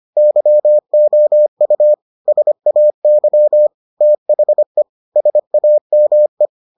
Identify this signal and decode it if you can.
Morse code, heard as a keyed tone, about 25 wpm, 600 Hz. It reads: YOU SAY THE SAME